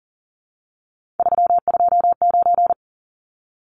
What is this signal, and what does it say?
Morse code, heard as a keyed tone, about 40 wpm, 700 Hz. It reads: 329
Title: Numbers - 3-Digit 40wpm